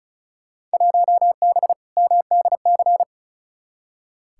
Morse code, heard as a keyed tone, about 34 wpm, 700 Hz. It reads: 1B MDC
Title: Field Day Exchanges - 35wpm